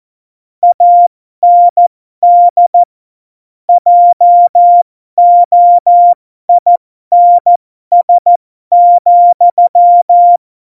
Morse code, heard as a keyed tone, about 14 wpm, spaced as for 12 wpm, 700 Hz. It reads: AND JOINS,
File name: Morsekurs / Witze